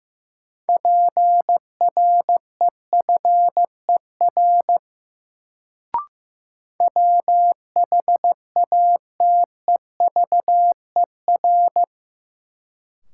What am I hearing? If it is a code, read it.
Morse code, heard as a keyed tone, about 15 wpm, 700 Hz. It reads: PREFER WHATEVER
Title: Sentences from Top 1100 Words - New Word Drill - 15wpm